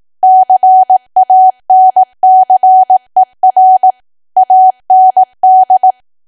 Morse code, heard as a keyed tone, about 18 wpm, 750 Hz. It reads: CANCER AND
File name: Archive / 18 WPM / 18WPM